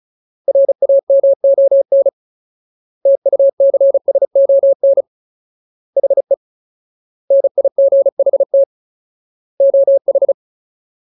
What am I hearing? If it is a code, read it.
Morse code, heard as a keyed tone, about 34 wpm, 550 Hz. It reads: RAMON TUCSON HE NIGHT OH
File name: Ultimate Word List / 35WPM Ultimate